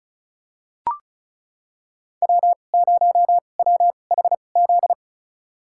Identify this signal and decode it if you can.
Morse code, heard as a keyed tone, about 34 wpm, spaced as for 25 wpm, 700 Hz. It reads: W0WHZ
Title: USA Call Signs - 5 Characters - 2x Character Spacing - 35wpm